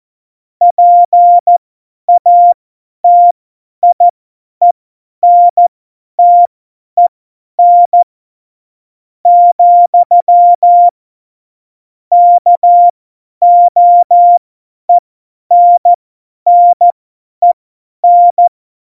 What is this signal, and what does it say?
Morse code, heard as a keyed tone, about 14 wpm, spaced as for 10 wpm, 700 Hz. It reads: PATIENTEN , KOENNEN
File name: Morsekurs / Witze / Witz 08. September 2020, 50 F70